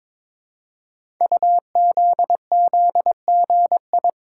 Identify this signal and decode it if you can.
Morse code, heard as a keyed tone, about 22 wpm, 700 Hz. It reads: UZZGI
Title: Code Groups - 5 Letters - 22wpm